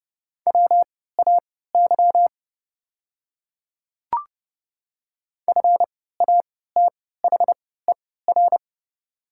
Morse code, heard as a keyed tone, about 30 wpm, spaced as for 17 wpm, 700 Hz. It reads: WAY FATHER
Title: 3x Character Spacing - Top 300 Words - 30wpm